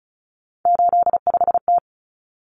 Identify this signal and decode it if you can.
Morse code, heard as a keyed tone, about 34 wpm, 700 Hz. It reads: 85T
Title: Common Cut Numbers - 3-Digit 35wpm